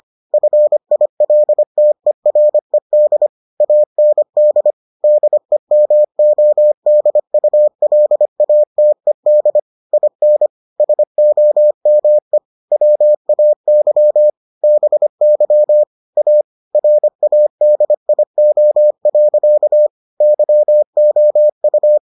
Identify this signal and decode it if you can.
Morse code, heard as a keyed tone, about 25 wpm, 600 Hz. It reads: FILTERED AND DEMODULATED IN SOME WAY BY A RADIO. YOU